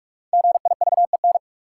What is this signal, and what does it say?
Morse code, heard as a keyed tone, about 46 wpm, 700 Hz. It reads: GIVEN
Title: Sentences from Top 1400 Words - New Word Drill Alt - 45wpm